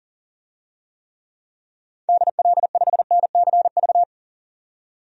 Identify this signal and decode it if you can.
Morse code, heard as a keyed tone, about 40 wpm, 700 Hz. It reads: DL5NCV
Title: DXCC-100 Call Signs - 6 Characters - 40wpm